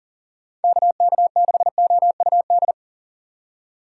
Morse code, heard as a keyed tone, about 39 wpm, 700 Hz. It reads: KK6OUD
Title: USA Call Signs - 6 Characters - 8x Character Spacing - 40wpm